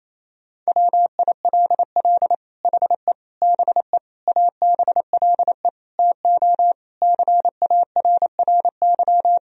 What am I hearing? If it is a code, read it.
Morse code, heard as a keyed tone, about 28 wpm, 700 Hz. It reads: WILL HE BE ABLE TO CARRY
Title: Sentences from Top 1000 Words - Review Part 3 - 28wpm